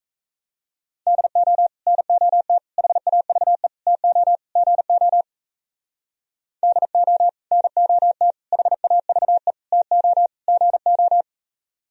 Morse code, heard as a keyed tone, about 32 wpm, 700 Hz. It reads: DO NOT HAVE TO GO DO NOT HAVE TO GO
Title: Speed-Racing - Sets of 5 Words 28wpm